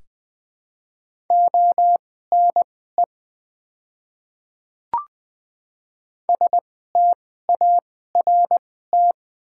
Morse code, heard as a keyed tone, about 20 wpm, spaced as for 14 wpm, 700 Hz. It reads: ONE START
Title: 2x Character Spacing - Top 300 Words - 20wpm